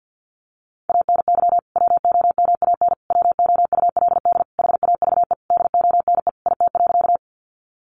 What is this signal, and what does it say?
Morse code, heard as a keyed tone, about 50 wpm, 700 Hz. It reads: ANY WOMAN WOULD HAVE DONE IT.
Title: Sentences from Top 1100 Words 50wpm